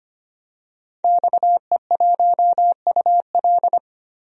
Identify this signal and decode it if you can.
Morse code, heard as a keyed tone, about 25 wpm, 700 Hz. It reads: XE1UL
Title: DXCC-100 Call Signs - 5 Characters - 25wpm